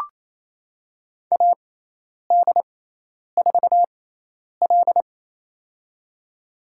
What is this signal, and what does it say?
Morse code, heard as a keyed tone, about 28 wpm, spaced as for 10 wpm, 700 Hz. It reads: AD4L
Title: USA Call Signs - 4 Characters - 6x Character Spacing - 28wpm